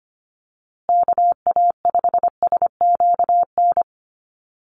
Morse code, heard as a keyed tone, about 25 wpm, 700 Hz. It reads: KA5SQN
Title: USA Call Signs Repeated 3x 25wpm